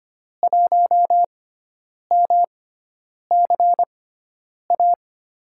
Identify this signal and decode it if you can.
Morse code, heard as a keyed tone, about 25 wpm, spaced as for 9 wpm, 700 Hz. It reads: 1MCA
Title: USA Call Signs - 6x Character Spacing - 25wpm